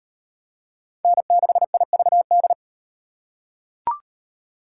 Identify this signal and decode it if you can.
Morse code, heard as a keyed tone, about 36 wpm, 700 Hz. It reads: N6IVD